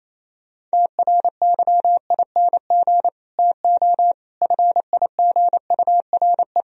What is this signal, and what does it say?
Morse code, heard as a keyed tone, about 38 wpm, 700 Hz. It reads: TRYING TO FIGURE